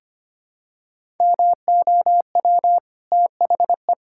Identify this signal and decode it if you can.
Morse code, heard as a keyed tone, about 25 wpm, 700 Hz. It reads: MOW THE
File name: Easy Sentences with Hints - 25wpm